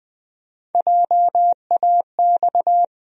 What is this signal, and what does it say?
Morse code, heard as a keyed tone, about 20 wpm, 700 Hz. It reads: JAX
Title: USA Call Sign Postfixes - 3 Letters 20wpm